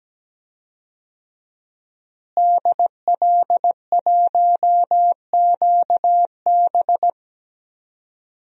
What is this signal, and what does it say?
Morse code, heard as a keyed tone, about 17 wpm, 700 Hz. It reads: DL1QB
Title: DXCC-100 Call Signs - 5 Characters - Repeated 2x - 17wpm